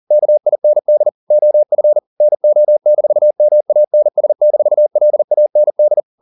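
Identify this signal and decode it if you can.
Morse code, heard as a keyed tone, about 40 wpm, 600 Hz. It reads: KIND OF NO-MANS-LAND
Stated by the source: Morse code at 40 wpm, practice text, CW med Gnister.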